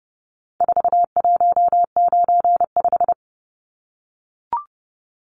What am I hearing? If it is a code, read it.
Morse code, heard as a keyed tone, about 30 wpm, 700 Hz. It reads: 4195
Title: Numbers - 4-Digit 30wpm